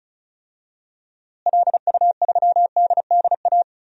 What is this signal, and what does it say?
Morse code, heard as a keyed tone, about 36 wpm, 700 Hz. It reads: LU3DDA
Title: DXCC-100 Call Signs - 6 Characters - Repeated 2x - 35wpm